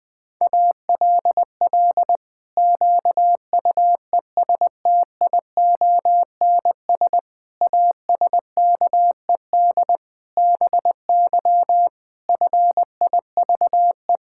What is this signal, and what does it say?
Morse code, heard as a keyed tone, about 20 wpm, 700 Hz. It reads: ALL QUESTIONS ASKED BY FIVE
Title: Pangram Challenge v2 - 20wpm